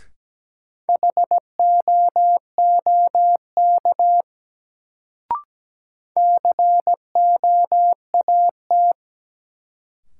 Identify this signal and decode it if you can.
Morse code, heard as a keyed tone, about 17 wpm, 700 Hz. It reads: HOOK COAT
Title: Sentences from Top 1600 Words - New Word Drill - 17wpm